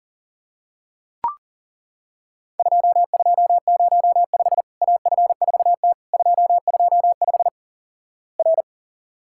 Morse code, heard as a keyed tone, about 38 wpm, 700 Hz. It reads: 2205 AF4T 225 R
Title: Worked All Europe DX Contest - QTC Exchange - 40wpm